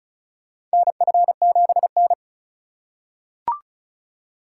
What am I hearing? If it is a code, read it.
Morse code, heard as a keyed tone, about 36 wpm, 700 Hz. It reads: NF7N
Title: USA Call Signs - 4 Characters - 4x Character Spacing - 35wpm